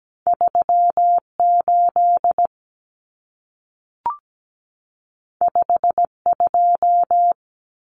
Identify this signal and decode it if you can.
Morse code, heard as a keyed tone, about 17 wpm, 700 Hz. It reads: 38 52
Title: Numbers - 2-Digit 17wpm